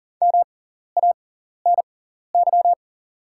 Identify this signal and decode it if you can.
Morse code, heard as a keyed tone, about 40 wpm, spaced as for 14 wpm, 700 Hz. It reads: MANY
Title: 6x Character Spacing - Top 100 Words - 40wpm